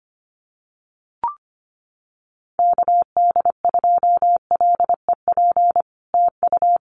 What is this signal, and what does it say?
Morse code, heard as a keyed tone, about 25 wpm, 700 Hz. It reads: KD2LEP TU
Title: Common QSO Phrases Encore 25wpm